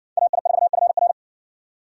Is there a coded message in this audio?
Yes: Morse code from 0.2 to 1.1 s, heard as a keyed tone, about 55 wpm, 700 Hz. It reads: AI4FR